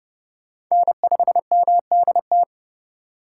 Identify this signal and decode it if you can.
Morse code, heard as a keyed tone, about 31 wpm, 700 Hz. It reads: N5MDT
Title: USA Call Signs - 5 Characters - 3x Character Spacing - 30wpm